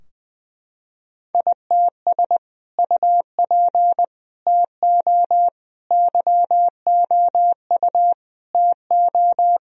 Morse code, heard as a keyed tone, about 20 wpm, 700 Hz. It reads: ITS UP TO YOU TO
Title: Sentences from Top 1000 Words - Review Part 2 - 20wpm